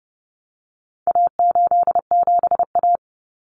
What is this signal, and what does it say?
Morse code, heard as a keyed tone, about 29 wpm, 700 Hz. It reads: A87A
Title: Common Cut Numbers - 4-Digit 30wpm